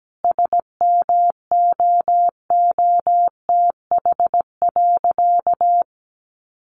Morse code, heard as a keyed tone, about 17 wpm, 700 Hz. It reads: SMOOTH.
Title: Sentences from Top 1400 Words 17wpm